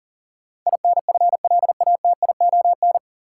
Morse code, heard as a keyed tone, about 40 wpm, 700 Hz. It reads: INFLATION
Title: Sentences from Top 1100 Words - New Word Drill Alt - 40wpm